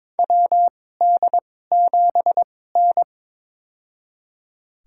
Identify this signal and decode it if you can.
Morse code, heard as a keyed tone, about 22 wpm, spaced as for 16 wpm, 700 Hz. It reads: WD7N
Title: USA Call Signs - 4 Characters - 2x Character Spacing - 22wpm